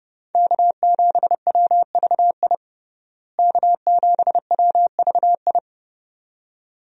Morse code, heard as a keyed tone, about 28 wpm, 700 Hz. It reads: K7WVI K7WVI
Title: USA Call Signs - 5 Characters - Repeated 2x - 30wpm